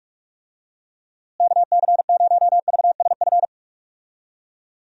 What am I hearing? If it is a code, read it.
Morse code, heard as a keyed tone, about 46 wpm, 700 Hz. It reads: KC0VSF
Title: USA Call Signs - 6 Characters - Repeated 2x - 45wpm